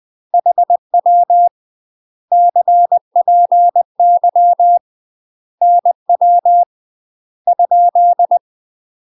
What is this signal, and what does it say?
Morse code, heard as a keyed tone, about 20 wpm, 700 Hz. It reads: HW CPY NW ?
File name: UK MASTER QSOs TRAINING / 20WPM / UK MASTER QSOs 0044